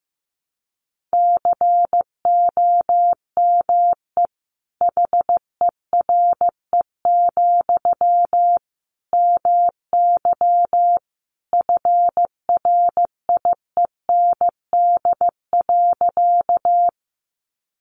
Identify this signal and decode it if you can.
Morse code, heard as a keyed tone, about 15 wpm, 700 Hz. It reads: COME HERE, MY FRIEND.
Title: Sentences from Top 300 Words 15wpm